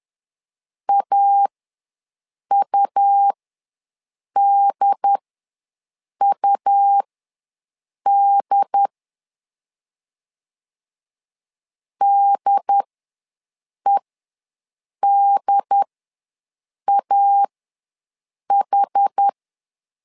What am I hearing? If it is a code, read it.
Morse code, heard as a keyed tone, about 10 wpm, spaced as for 6 wpm, 800 Hz. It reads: AUDUD DEDAH
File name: LEZIONI / 2 UHDN / 2 UHD ws 10wpm 2058